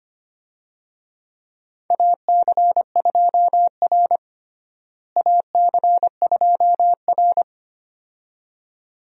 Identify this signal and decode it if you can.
Morse code, heard as a keyed tone, about 25 wpm, 700 Hz. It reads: AC2R AC2R
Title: USA Call Signs - 4 Characters - Repeated 2x - 25wpm